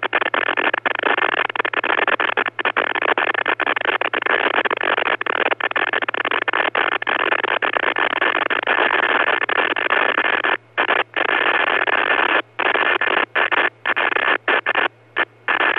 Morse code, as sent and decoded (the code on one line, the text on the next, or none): none